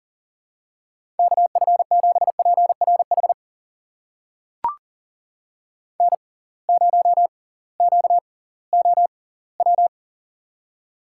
Morse code, heard as a keyed tone, about 40 wpm, spaced as for 22 wpm, 700 Hz. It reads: KF7PRH N0QOW
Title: USA Call Signs - 6x Character Spacing - 40wpm